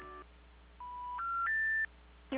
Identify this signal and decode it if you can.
Special information tone.
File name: recordings / PortlandUnion Millennium